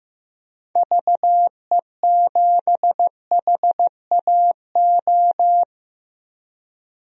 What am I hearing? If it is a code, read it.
Morse code, heard as a keyed tone, about 15 wpm, 700 Hz. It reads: VE7HAO
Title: DXCC-100 Call Signs - 6 Characters - 15wpm